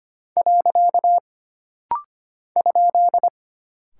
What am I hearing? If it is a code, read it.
Morse code, heard as a keyed tone, about 25 wpm, 700 Hz. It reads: . ?
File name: Single Punctuation-Prosign - Rapid-Fire - 25wpm